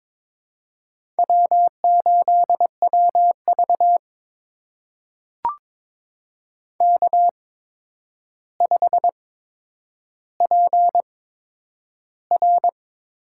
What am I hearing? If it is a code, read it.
Morse code, heard as a keyed tone, about 23 wpm, spaced as for 10 wpm, 700 Hz. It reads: W8WV K5PR
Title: USA Call Signs - 8x Character Spacing - 22wpm